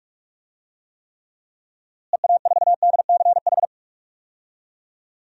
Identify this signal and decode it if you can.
Morse code, heard as a keyed tone, about 46 wpm, 700 Hz. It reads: EA4DCH
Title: DXCC-100 Call Signs - 6 Characters - Repeated 2x - 45wpm